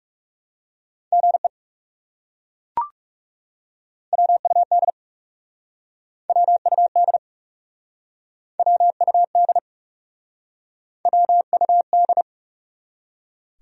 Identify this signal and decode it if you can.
Morse code, heard as a keyed tone, about 38 wpm, 700 Hz. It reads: GE WUD WUD WUD WUD